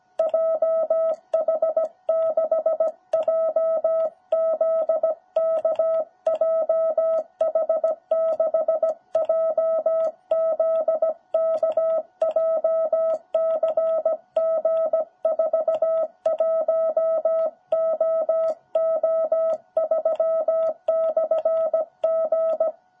Morse code, heard as a keyed tone, about 17 wpm, 660 Hz. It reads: JH6JZK JH6JZK JCG41OO3/G
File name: jh8cbh / 2021cw